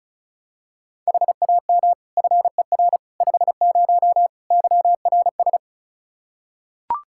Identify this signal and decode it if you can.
Morse code, heard as a keyed tone, about 34 wpm, 700 Hz. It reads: HAM FER 50 YRS
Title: Practice QSO - Exchange 3 - More Info - 35wpm